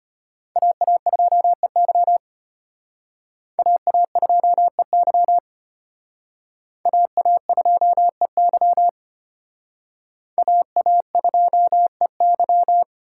Morse code, heard as a keyed tone, about 31 wpm, 700 Hz. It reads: AA2EY AA2EY AA2EY AA2EY
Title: Speed Racing - USA 2x2 Call Signs 25wpm